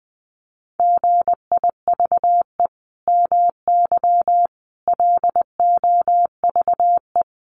Morse code, heard as a keyed tone, about 19 wpm, 700 Hz. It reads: GIVE MY LOVE
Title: Sentences from Top 1400 Words 20wpm